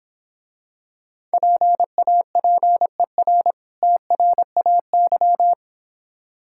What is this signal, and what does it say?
Morse code, heard as a keyed tone, about 26 wpm, 700 Hz. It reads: PAPER TRAY